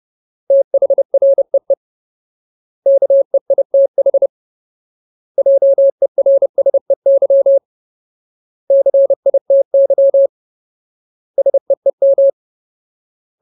Morse code, heard as a keyed tone, about 30 wpm, 550 Hz. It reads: THREE KEITH JERSEY CITY SEEM
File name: Ultimate Word List / 30WPM Ultimate